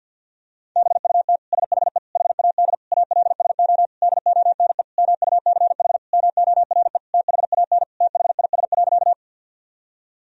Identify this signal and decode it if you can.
Morse code, heard as a keyed tone, about 48 wpm, 700 Hz. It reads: BUT SHE HAD ALSO DONE MUCH MORE THAN THIS.
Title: Sentences from Top 1000 Words - Review Part 1 - 50wpm